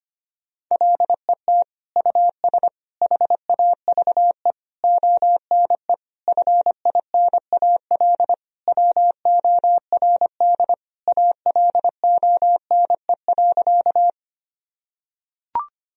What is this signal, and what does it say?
Morse code, heard as a keyed tone, about 25 wpm, 700 Hz. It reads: LET US HAVE ONE FINAL WORD ALONE.
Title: Sentences from Top 1000 Words - Review Part 2 - 25wpm